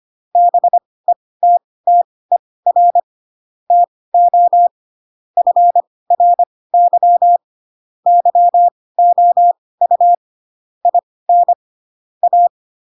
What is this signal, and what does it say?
Morse code, heard as a keyed tone, about 25 wpm, spaced as for 18 wpm, 700 Hz. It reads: BETTER TO FRY YOU IN A
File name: Pinocchio / Pinocchio ch5 12 25 ik1hge